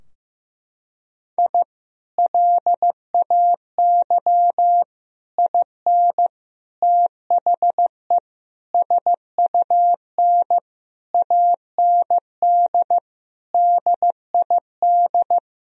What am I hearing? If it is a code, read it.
Morse code, heard as a keyed tone, about 15 wpm, 700 Hz. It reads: I LAY IN THE SUN AND DID
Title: Sentences from Top 1000 Words - Review Part 2 - 15wpm